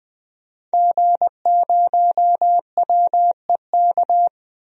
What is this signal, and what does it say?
Morse code, heard as a keyed tone, about 21 wpm, 700 Hz. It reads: G0WEK
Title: DXCC-100 Call Signs - 5 Characters - Repeated 2x - 20wpm